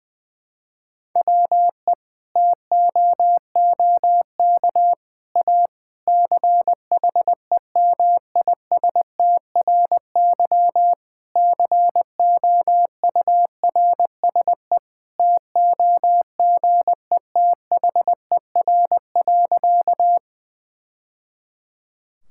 Morse code, heard as a keyed tone, about 21 wpm, 700 Hz. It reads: WE TOOK A CHEMISTRY COURSE TOGETHER.
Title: Sentences from Top 1000 Words - Review Part 6 - 20wpm